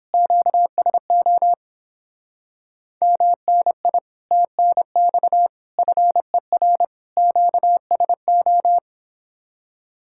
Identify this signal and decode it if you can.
Morse code, heard as a keyed tone, about 26 wpm, 700 Hz. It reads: QSO MNI TNX FER QSO
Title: Speed-Racing Common QSO Phrases - 22wpm